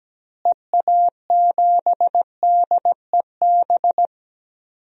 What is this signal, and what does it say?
Morse code, heard as a keyed tone, about 17 wpm, 700 Hz. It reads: EA7DEB
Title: DXCC-100 Call Signs - 6 Characters - Repeated 2x - 17wpm